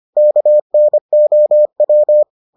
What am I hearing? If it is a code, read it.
Morse code, heard as a keyed tone, about 25 wpm, 600 Hz. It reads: KNOW